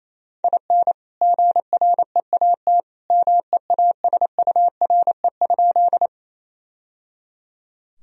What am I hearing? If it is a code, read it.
Morse code, heard as a keyed tone, about 28 wpm, 700 Hz. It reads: IN GREAT MEASURE?
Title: Sentences from Top 1000 Words - Review Part 2 - 28wpm